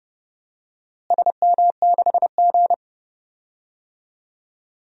Morse code, heard as a keyed tone, about 30 wpm, 700 Hz. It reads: SM6G
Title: DXCC-100 Call Signs - 4 Characters - Repeated 2x - 30wpm